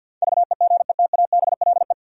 Morse code, heard as a keyed tone, about 51 wpm, 700 Hz. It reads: VEGETABLE
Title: Sentences from Top 1300 Words - New Word Drill - 50wpm